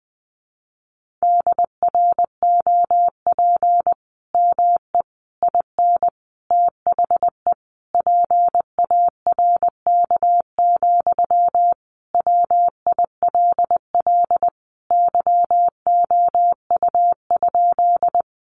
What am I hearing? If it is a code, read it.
Morse code, heard as a keyed tone, about 20 wpm, 700 Hz. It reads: DROP ME IN THE PARK, WILL YOU?
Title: Sentences from Top 1400 Words 20wpm